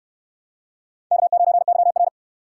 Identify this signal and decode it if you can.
Morse code, heard as a keyed tone, about 65 wpm, 700 Hz. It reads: K9YR